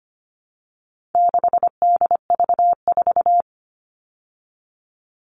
Morse code, heard as a keyed tone, about 25 wpm, 700 Hz. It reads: 6DV4
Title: All Cut Numbers - 4-Digit 25wpm